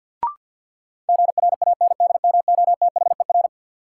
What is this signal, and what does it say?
Morse code, heard as a keyed tone, about 51 wpm, 700 Hz. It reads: GRANDMOTHER
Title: Sentences from Top 1200 Words - New Word Drill - 50wpm